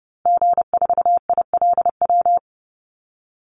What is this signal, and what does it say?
Morse code, heard as a keyed tone, about 30 wpm, 700 Hz. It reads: G4ILW
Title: Call Signs Repeated 2x - 30wpm